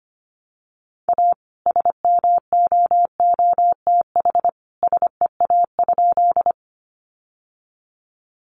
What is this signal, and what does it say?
Morse code, heard as a keyed tone, about 25 wpm, 700 Hz. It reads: A SMOOTH SEA?
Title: Sentences from Top 1400 Words 25wpm